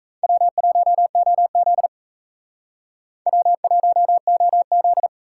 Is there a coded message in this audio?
Yes: Morse code from 0.2 to 5.1 s, heard as a keyed tone, about 35 wpm, 700 Hz. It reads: W1OZ W1OZ